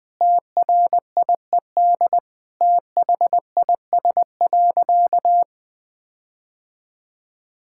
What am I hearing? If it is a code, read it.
Morse code, heard as a keyed tone, about 20 wpm, 700 Hz. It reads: TRIED THIS.
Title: Sentences with 6-Word Phrases - 20wpm